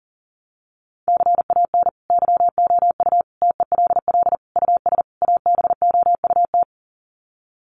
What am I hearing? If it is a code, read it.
Morse code, heard as a keyed tone, about 37 wpm, 700 Hz. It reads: CAN YOU TELL US ABOUT
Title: Sets of 5 Words Encore2 40wpm